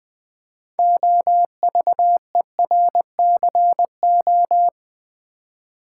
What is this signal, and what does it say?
Morse code, heard as a keyed tone, about 20 wpm, 700 Hz. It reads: OVERCO
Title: Build A Word - 2 Letters At A Time - 8-Letter Words - 20wpm